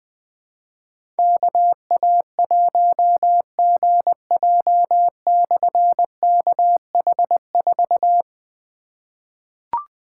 Morse code, heard as a keyed tone, about 20 wpm, 700 Hz. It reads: KA1GJ/KH4
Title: Difficult Call Signs 20wpm